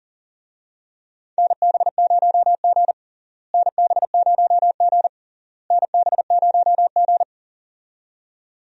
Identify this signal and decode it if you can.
Morse code, heard as a keyed tone, about 40 wpm, 700 Hz. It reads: NB0G NB0G NB0G
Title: USA Call Signs - 4 Characters - Repeated 3x - 40wpm